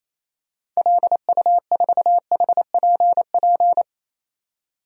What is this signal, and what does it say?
Morse code, heard as a keyed tone, about 28 wpm, 700 Hz. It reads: LU4HPP
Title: DXCC-100 Call Signs - 6 Characters - 28wpm